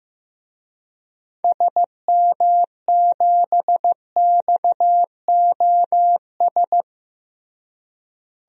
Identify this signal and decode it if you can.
Morse code, heard as a keyed tone, about 16 wpm, 700 Hz. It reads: SM7XOS
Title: DXCC-100 Call Signs - 6 Characters - Repeated 2x - 15wpm